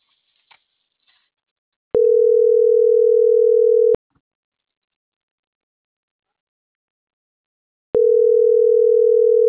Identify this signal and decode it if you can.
Ringback tone (North America), 440 + 480 Hz.